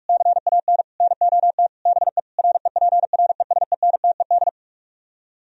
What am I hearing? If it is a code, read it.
Morse code, heard as a keyed tone, about 44 wpm, 700 Hz. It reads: CAN NOT BE REPRESENTED